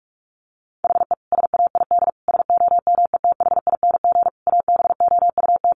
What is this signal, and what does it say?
Morse code, heard as a keyed tone, about 44 wpm, 700 Hz. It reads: HE SAID SOMETHING ABOUT
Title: Sentences from Top 1500 Words 45wpm